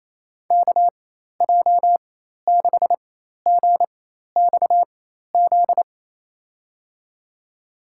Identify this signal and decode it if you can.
Morse code, heard as a keyed tone, about 28 wpm, spaced as for 13 wpm, 700 Hz. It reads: KJ6GXZ
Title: USA Call Signs - 6 Characters - 4x Character Spacing - 28wpm